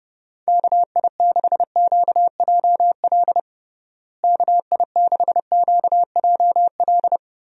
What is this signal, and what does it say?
Morse code, heard as a keyed tone, about 32 wpm, 700 Hz. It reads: KI6QJL KI6QJL
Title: USA Call Signs - 6 Characters - Repeated 2x - 30wpm